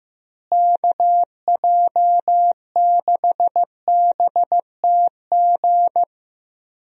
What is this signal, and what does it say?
Morse code, heard as a keyed tone, about 15 wpm, 700 Hz. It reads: KJ6BTG
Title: USA Call Signs - 6 Characters - Repeated 3x - 15wpm